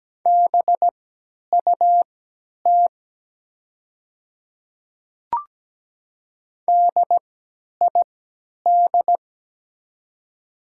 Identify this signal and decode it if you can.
Morse code, heard as a keyed tone, about 17 wpm, spaced as for 10 wpm, 700 Hz. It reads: BUT DID
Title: 3x Character Spacing - Top 100 Words - 17wpm